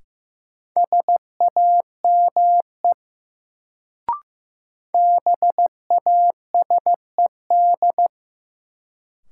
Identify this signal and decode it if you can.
Morse code, heard as a keyed tone, about 15 wpm, 700 Hz. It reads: SAME BASED
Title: Sets of 2 Words - New Word Drill - 15wpm